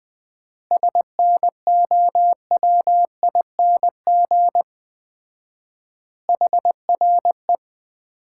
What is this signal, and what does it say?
Morse code, heard as a keyed tone, about 21 wpm, 700 Hz. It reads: SNOWING HRE
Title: Common QSO Phrases - 4x Word-Spacing - 20wpm